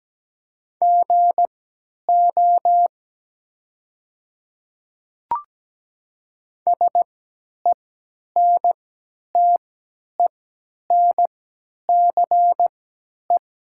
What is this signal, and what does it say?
Morse code, heard as a keyed tone, about 17 wpm, spaced as for 10 wpm, 690 Hz. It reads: GO SENTENCE
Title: 3x Character Spacing - Top 300 Words - 17wpm